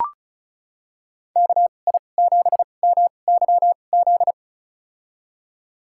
Morse code, heard as a keyed tone, about 36 wpm, spaced as for 25 wpm, 700 Hz. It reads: KI7MYZ
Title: USA Call Signs - 6 Characters - 2x Character Spacing - 35wpm